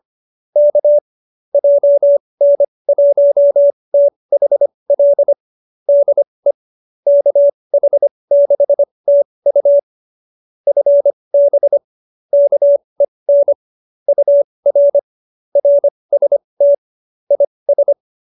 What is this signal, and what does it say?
Morse code, heard as a keyed tone, about 25 wpm, spaced as for 20 wpm, 570 Hz. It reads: K JN1THL DE KH6TU FB KEN UR RST IS
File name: QSO309 20